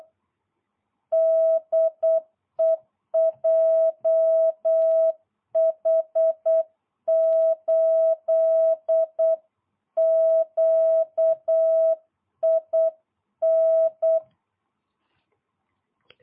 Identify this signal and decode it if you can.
Morse code, heard as a keyed tone, about 8 wpm, 660 Hz. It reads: DE JH8QIN